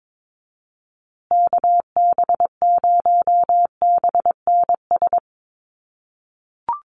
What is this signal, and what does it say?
Morse code, heard as a keyed tone, about 22 wpm, 700 Hz. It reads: KB0BNS
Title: USA Call Signs Repeated 2x 22wpm